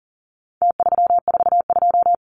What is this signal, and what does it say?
Morse code, heard as a keyed tone, about 40 wpm, 700 Hz. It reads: T342
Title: Numbers - Contest Style - 4-Digit 40wpm